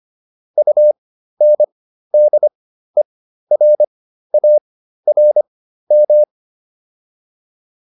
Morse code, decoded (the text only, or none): UNDERARM